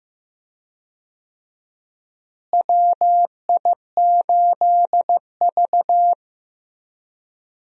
Morse code, heard as a keyed tone, about 15 wpm, 700 Hz. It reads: WI8V
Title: USA Call Signs - 4 Characters - Repeated 3x - 15wpm